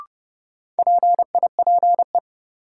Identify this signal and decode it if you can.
Morse code, heard as a keyed tone, about 32 wpm, 700 Hz. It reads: PIPE
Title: Sentences from Top 1500 Words - New Word Drill - 30wpm